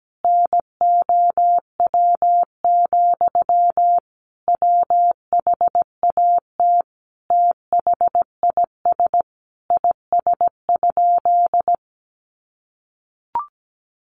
Morse code, heard as a keyed tone, about 17 wpm, 700 Hz. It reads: NOW, WHAT THIS IS?
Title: Sentences from Top 1500 Words 17wpm